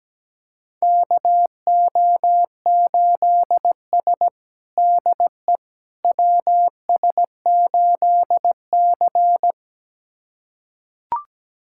Morse code, heard as a keyed tone, about 17 wpm, 700 Hz. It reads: KO8S DE WS8C
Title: USA Callsign DE Callsign - 4 Characters - 17wpm